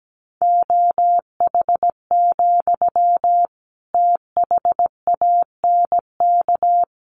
Morse code, heard as a keyed tone, about 17 wpm, 700 Hz. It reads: OH, THANK
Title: Sentences from Top 600 Words 17wpm